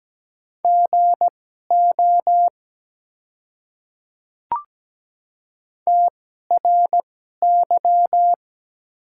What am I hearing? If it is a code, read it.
Morse code, heard as a keyed tone, about 17 wpm, spaced as for 12 wpm, 700 Hz. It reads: GO TRY
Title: 2x Character Spacing - Top 200 Words - 17wpm